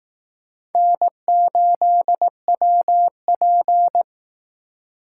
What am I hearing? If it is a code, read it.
Morse code, heard as a keyed tone, about 19 wpm, 700 Hz. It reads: N8WP